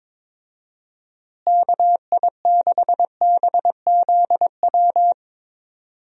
Morse code, heard as a keyed tone, about 22 wpm, 700 Hz. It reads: KI6BZW
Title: USA Call Signs - 6 Characters - Repeated 3x - 22wpm